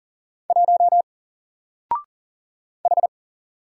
Morse code, heard as a keyed tone, about 40 wpm, 700 Hz. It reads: 1 H
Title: Single Letter-Number 40wpm